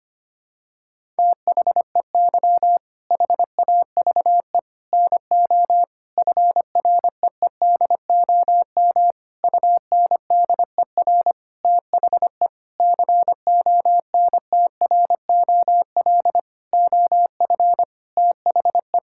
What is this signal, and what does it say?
Morse code, heard as a keyed tone, about 25 wpm, 700 Hz. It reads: THEY HAVE NO FREEDOM UNDER THE CONTROL OF THE
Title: Sentences from Top 1000 Words - Review Part 5 - 25wpm